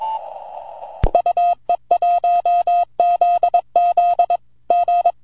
Morse code, heard as a keyed tone, about 24 wpm, 700 Hz. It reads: UE1ZZ G
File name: vq9la / VE1ZZ 10AUG08 0104Z 160mtr